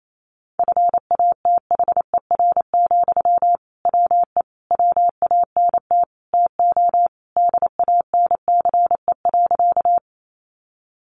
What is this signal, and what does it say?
Morse code, heard as a keyed tone, about 28 wpm, 700 Hz. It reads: FATHER, WE WANT TO DANCE.
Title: Sentences from Top 700 Words 28wpm